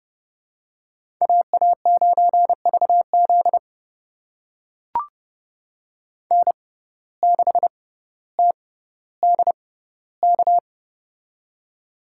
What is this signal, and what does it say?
Morse code, heard as a keyed tone, about 30 wpm, spaced as for 15 wpm, 700 Hz. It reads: AA9VZ N6TDK
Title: USA Call Signs - 5 Characters - 6x Character Spacing - 30wpm